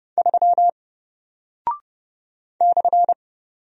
Morse code, heard as a keyed tone, about 30 wpm, 700 Hz. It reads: 3 /
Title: Single Character (Letter:Number:Punctuation:Prosign) 30wpm